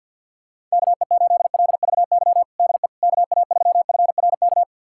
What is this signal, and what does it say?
Morse code, heard as a keyed tone, about 51 wpm, 700 Hz. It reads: KE8LVY DE KA3FRK
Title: USA Callsign DE Callsign - 6 Characters - 50wpm